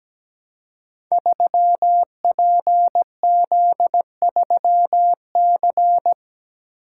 Morse code, heard as a keyed tone, about 17 wpm, 700 Hz. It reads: 3PZ3C
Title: Code Groups - 5 Letter:Numbers - 17wpm